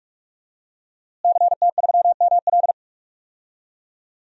Morse code, heard as a keyed tone, about 44 wpm, 700 Hz. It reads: CT3ML
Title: DXCC-100 Call Signs - 5 Characters - Repeated 3x - 45wpm